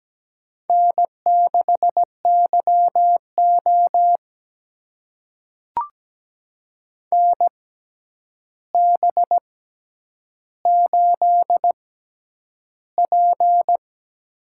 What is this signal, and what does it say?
Morse code, heard as a keyed tone, about 17 wpm, spaced as for 9 wpm, 700 Hz. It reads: N6YO NB8P
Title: USA Call Signs - 4 Characters - 6x Character Spacing - 17wpm